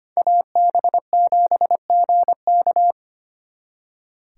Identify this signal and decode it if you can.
Morse code, heard as a keyed tone, about 32 wpm, 700 Hz. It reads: AB7GK